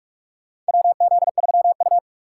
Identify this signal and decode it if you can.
Morse code, heard as a keyed tone, about 46 wpm, 700 Hz. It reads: WZ3U
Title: USA Call Signs - 4 Characters - 6x Character Spacing - 45wpm